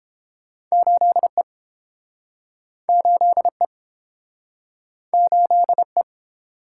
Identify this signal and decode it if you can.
Morse code, heard as a keyed tone, about 32 wpm, 700 Hz. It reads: 8E 8E 8E